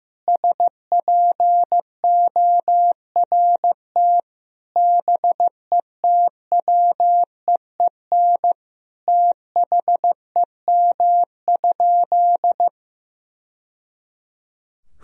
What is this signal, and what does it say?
Morse code, heard as a keyed tone, about 15 wpm, 700 Hz. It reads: SPORT BETWEEN THEM?
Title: Sentences from Top 1000 Words - Review Part 4 - 15wpm